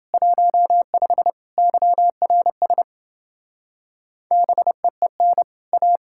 Morse code, heard as a keyed tone, about 30 wpm, 700 Hz. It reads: 15 YRS BEEN A